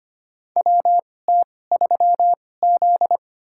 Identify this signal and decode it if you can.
Morse code, heard as a keyed tone, about 25 wpm, spaced as for 18 wpm, 700 Hz. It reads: WT3Z
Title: USA Call Signs - 2x Character Spacing - 25wpm